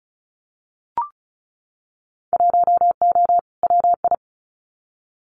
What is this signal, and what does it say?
Morse code, heard as a keyed tone, about 36 wpm, 700 Hz. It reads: 1O WI
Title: Winter Field Day Exchanges 35wpm